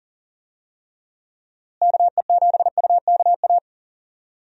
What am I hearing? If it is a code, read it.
Morse code, heard as a keyed tone, about 40 wpm, 700 Hz. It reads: KE7UKA
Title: USA Call Signs - 6 Characters - Repeated 2x - 40wpm